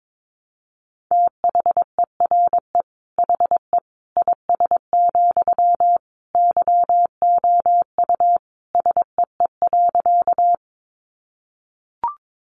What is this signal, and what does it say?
Morse code, heard as a keyed tone, about 22 wpm, 700 Hz. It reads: THERE HE IS, YOU SEE.
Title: Sentences from Top 100 Words 22wpm